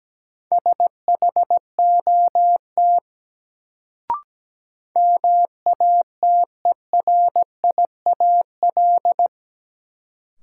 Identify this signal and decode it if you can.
Morse code, heard as a keyed tone, about 17 wpm, 700 Hz. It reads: SHOT MATERIAL
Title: Sentences from Top 1100 Words - New Word Drill Alt - 17wpm